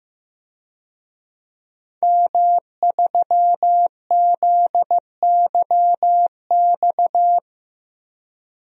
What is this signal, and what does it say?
Morse code, heard as a keyed tone, about 15 wpm, 700 Hz. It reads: M3ZYX
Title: DXCC-100 Call Signs - 5 Characters - 15wpm